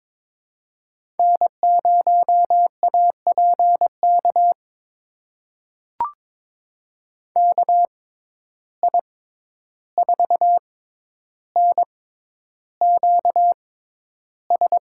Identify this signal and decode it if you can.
Morse code, heard as a keyed tone, about 22 wpm, spaced as for 11 wpm, 700 Hz. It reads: N0APK KI4NQS
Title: USA Call Signs - 6x Character Spacing - 22wpm